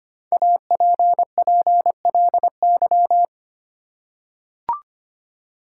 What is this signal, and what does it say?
Morse code, heard as a keyed tone, about 25 wpm, 700 Hz. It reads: APPLY
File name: 5 Letter Words - First 3 Letters Given - 25wpm